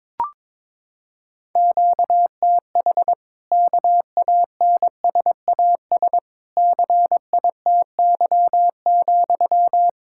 Morse code, heard as a keyed tone, about 22 wpm, 700 Hz. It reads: QTH KANSAS CITY,
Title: Practice QSO - Exchange 1 - Beginning - 22wpm